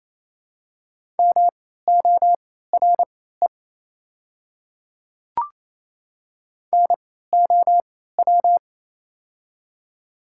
Morse code, heard as a keyed tone, about 27 wpm, spaced as for 16 wpm, 690 Hz. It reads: MORE NOW
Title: 3x Character Spacing - Top 500 Words - 28wpm